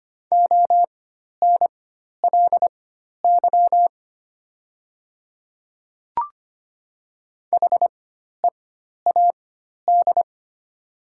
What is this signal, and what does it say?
Morse code, heard as a keyed tone, about 25 wpm, spaced as for 12 wpm, 700 Hz. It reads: ONLY HEAD
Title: 4x Character Spacing - Top 300 Words - 25wpm